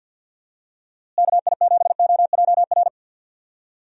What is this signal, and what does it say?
Morse code, heard as a keyed tone, about 51 wpm, 700 Hz. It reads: KI7OJR
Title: USA Call Signs - 6 Characters - Repeated 2x - 50wpm